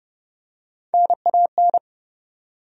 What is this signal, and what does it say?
Morse code, heard as a keyed tone, about 30 wpm, 700 Hz. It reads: NAN
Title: USA Common Names - Max 4 Letters - 30wpm